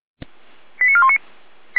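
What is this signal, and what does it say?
Five-tone selective call (ZVEI), heard as quick runs of five tones, tones of about 70 ms. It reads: 89419
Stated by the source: ZVEI five-tone selective call received on SDR.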